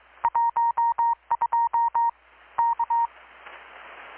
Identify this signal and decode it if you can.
Morse code, heard as a keyed tone, about 23 wpm, 930 Hz. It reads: 12 K
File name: blog / swl / 6-abril-2019-12695-USB-KPS